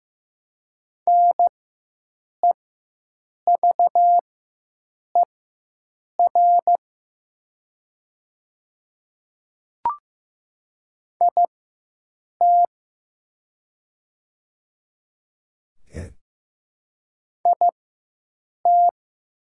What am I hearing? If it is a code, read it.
Morse code, heard as a keyed tone, about 15 wpm, spaced as for 7 wpm, 700 Hz. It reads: NEVER IT IT
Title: 4x Character Spacing - Top 300 Words - 15wpm